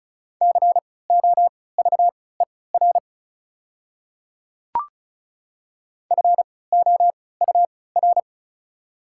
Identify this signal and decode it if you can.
Morse code, heard as a keyed tone, about 34 wpm, spaced as for 20 wpm, 700 Hz. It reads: COVER FOUR
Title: 3x Character Spacing - Top 300 Words - 35wpm